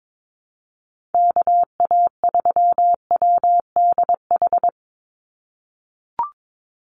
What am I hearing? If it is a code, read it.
Morse code, heard as a keyed tone, about 22 wpm, 700 Hz. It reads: KA3WDH
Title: USA Call Signs 22wpm